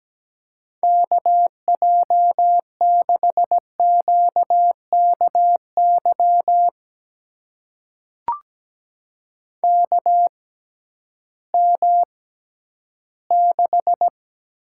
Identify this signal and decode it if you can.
Morse code, heard as a keyed tone, about 17 wpm, spaced as for 11 wpm, 700 Hz. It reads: KJ6QKY KM6
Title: USA Call Signs - 6 Characters - 6x Character Spacing - 17wpm